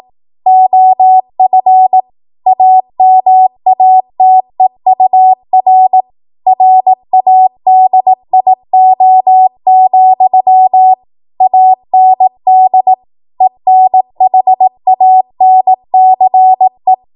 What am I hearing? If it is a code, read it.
Morse code, heard as a keyed tone, about 18 wpm, 750 Hz. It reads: OF AMATEUR RADIO, AND ENHANCE
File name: Morse / 18WPM